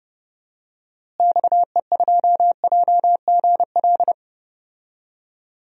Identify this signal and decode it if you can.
Morse code, heard as a keyed tone, about 29 wpm, 700 Hz. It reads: XE2JGL
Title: DXCC-100 Call Signs - 6 Characters - Repeated 3x - 30wpm